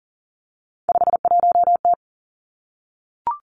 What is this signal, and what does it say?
Morse code, heard as a keyed tone, about 41 wpm, 690 Hz. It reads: 51T
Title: Three Challenging Letter-Number Combinations 40wpm